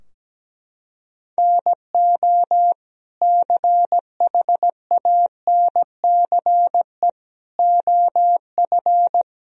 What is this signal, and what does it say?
Morse code, heard as a keyed tone, about 17 wpm, 700 Hz. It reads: NO CHANCE OF
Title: Sentences from Top 1000 Words - Review Part 2 - 17wpm